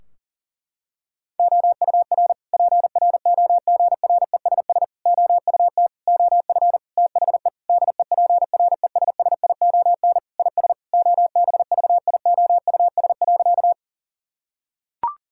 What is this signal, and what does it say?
Morse code, heard as a keyed tone, about 40 wpm, 700 Hz. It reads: OUR PROGRESS OUT OF THE DEPRESSION IS OBVIOUS.
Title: Sentences from Top 1600 Words 40wpm